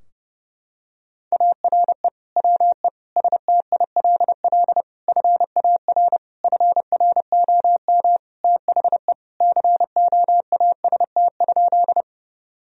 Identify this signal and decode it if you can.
Morse code, heard as a keyed tone, about 30 wpm, 700 Hz. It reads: ARE WE STILL FAR FROM THE COAST?
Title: Sentences from Top 1000 Words - Review Part 4 - 30wpm